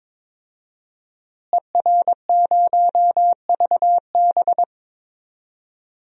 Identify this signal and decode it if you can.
Morse code, heard as a keyed tone, about 21 wpm, 700 Hz. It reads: ER0VB
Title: DXCC-100 Call Signs - 5 Characters - Repeated 3x - 22wpm